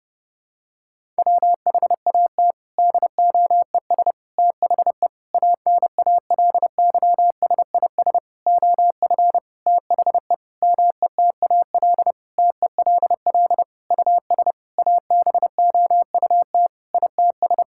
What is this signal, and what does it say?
Morse code, heard as a keyed tone, about 30 wpm, 700 Hz. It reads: WHAT DOES THE ANALYSIS OF THE METAL TELL US ABOUT ITS
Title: Sentences from Top 1000 Words - Review Part 6 - 30wpm